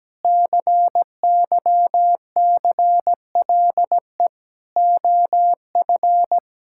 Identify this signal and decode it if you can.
Morse code, heard as a keyed tone, about 17 wpm, 700 Hz. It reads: CYCLE OF
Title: Sentences from Top 1000 Words - Review Part 6 - 17wpm